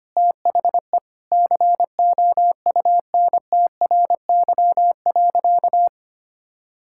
Morse code, heard as a keyed tone, about 25 wpm, 700 Hz. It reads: THE COUNTRY.
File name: practice / 25wpm / Sentences with 2-Word Phrases - 25wpm